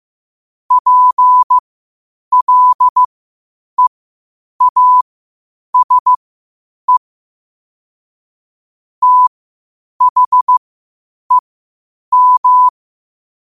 Morse code, heard as a keyed tone, about 15 wpm, spaced as for 8 wpm, 1000 Hz. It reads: PLEASE THEM
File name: qotdfiles / 8wpm